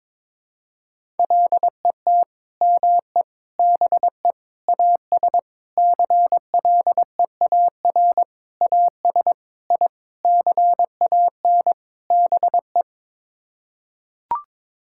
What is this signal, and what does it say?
Morse code, heard as a keyed tone, about 22 wpm, 700 Hz. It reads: LET ME BE AS CLEAR AS I CAN BE
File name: Sets of 9 Words - 22wpm